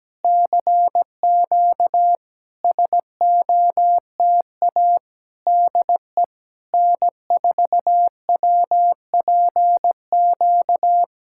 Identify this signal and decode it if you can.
Morse code, heard as a keyed tone, about 17 wpm, 700 Hz. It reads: CQ SOTA DE N4WPQ
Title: SOTA QSO Practice 17wpm